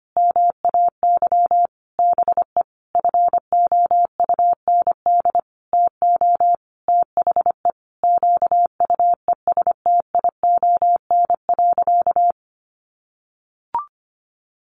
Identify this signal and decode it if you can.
Morse code, heard as a keyed tone, about 25 wpm, 700 Hz. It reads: MAY BE FOUND TO THE QUESTION.
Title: Sentences from Top 1400 Words 25wpm